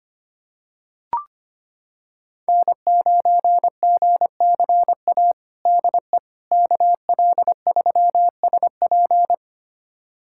Morse code, heard as a keyed tone, about 25 wpm, 700 Hz. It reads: N9GCA DE KL3SP
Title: USA Callsign DE Callsign - 5 Characters - 25wpm